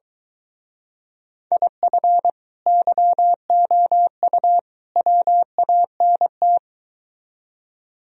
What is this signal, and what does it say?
Morse code, heard as a keyed tone, about 22 wpm, 700 Hz. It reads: IF YOU WANT